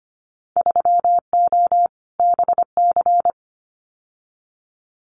Morse code, heard as a keyed tone, about 25 wpm, 700 Hz. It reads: 3O BC
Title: Winter Field Day Exchanges Encore 25wpm